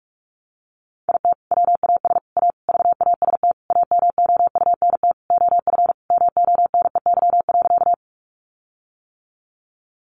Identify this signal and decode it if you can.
Morse code, heard as a keyed tone, about 44 wpm, 700 Hz. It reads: IT WAS A VAST AMOUNT OF MONEY.
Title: Sentences from Top 1300 Words 45wpm